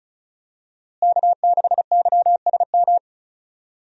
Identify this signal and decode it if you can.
Morse code, heard as a keyed tone, about 35 wpm, 700 Hz. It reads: K6YSM
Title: USA Call Signs - 5 Characters - Repeated 3x - 35wpm